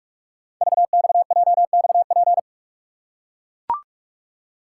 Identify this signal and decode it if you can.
Morse code, heard as a keyed tone, about 44 wpm, 700 Hz. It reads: VXJXP
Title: Code Groups - 5 Letter:Numbers - 45wpm